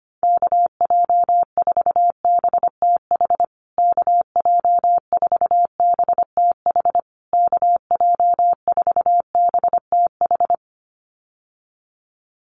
Morse code, heard as a keyed tone, about 25 wpm, 700 Hz. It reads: KJ4BTH KJ4BTH KJ4BTH
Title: USA Call Signs Repeated 3x 25wpm